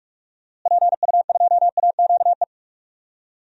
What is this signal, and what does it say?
Morse code, heard as a keyed tone, about 46 wpm, 700 Hz. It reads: PU2AQE
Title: DXCC-100 Call Signs - 6 Characters - Repeated 2x - 45wpm